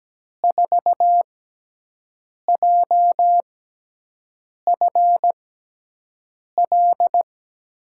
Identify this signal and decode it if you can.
Morse code, heard as a keyed tone, about 17 wpm, spaced as for 6 wpm, 700 Hz. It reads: 4JFL
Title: USA Call Signs - 6x Character Spacing - 17wpm